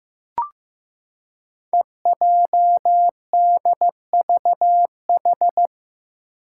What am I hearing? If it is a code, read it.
Morse code, heard as a keyed tone, about 15 wpm, 700 Hz. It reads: EJDVH
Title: Five Challenging Letter-Number Combinations 15wpm